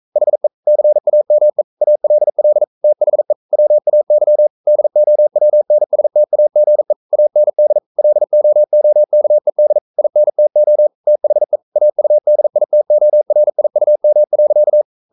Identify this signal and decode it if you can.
Morse code, heard as a keyed tone, about 42 wpm, 600 Hz. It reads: HE CAME ALL THE WAY DOWNSTAGE AND LOOKED INTO THE AUDITORIUM.
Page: Romanen Never af Ken Follet på engelsk.